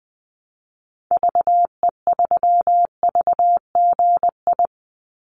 Morse code, heard as a keyed tone, about 20 wpm, 700 Hz. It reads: VE3VGI
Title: Call Signs Repeated 2x - 20wpm